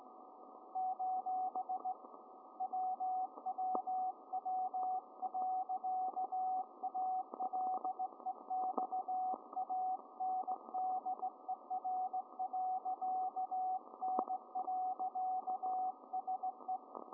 Morse code, heard as a keyed tone, about 18 wpm, 750 Hz, with radio noise. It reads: : WWW.ALEXANDER.N.SE